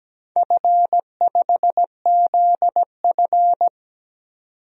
Morse code, heard as a keyed tone, about 17 wpm, 700 Hz. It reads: F5ZF
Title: DXCC-100 Call Signs - 4 Characters - 17wpm